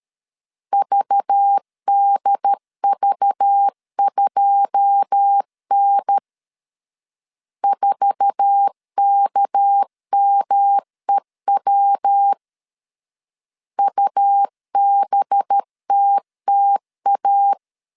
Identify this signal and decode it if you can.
Morse code, heard as a keyed tone, about 13 wpm, 800 Hz. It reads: VDV2N 4KMEW UBTTA
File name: CW / DET / 1 9 1500C S A 12wpm 3800